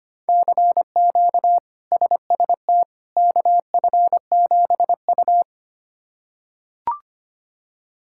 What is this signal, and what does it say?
Morse code, heard as a keyed tone, about 25 wpm, 700 Hz. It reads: CQ SST KF7U
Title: K1USN SST QSO Practice - 25wpm